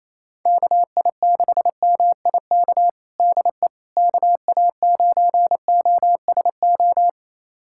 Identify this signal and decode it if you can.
Morse code, heard as a keyed tone, about 28 wpm, 700 Hz. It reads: KI6MIK DE KA9OSO
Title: USA Callsign DE Callsign - 6 Characters - 28wpm